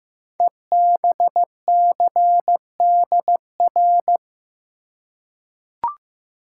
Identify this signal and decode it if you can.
Morse code, heard as a keyed tone, about 15 wpm, 700 Hz. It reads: EBCDR
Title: Code Groups - 5 Letters - 15wpm